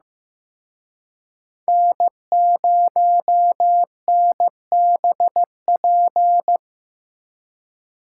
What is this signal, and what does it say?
Morse code, heard as a keyed tone, about 18 wpm, 700 Hz. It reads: N0NBP
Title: Speed Racing - USA 1x3 Call Signs 15wpm